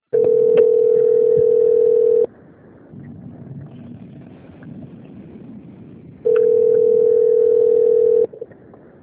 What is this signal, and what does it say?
Ringback tone (North America), 440 + 480 Hz.